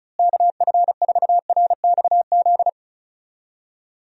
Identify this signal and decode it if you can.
Morse code, heard as a keyed tone, about 34 wpm, 700 Hz. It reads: KF4RXZ
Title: USA Call Signs - 6 Characters - Repeated 3x - 35wpm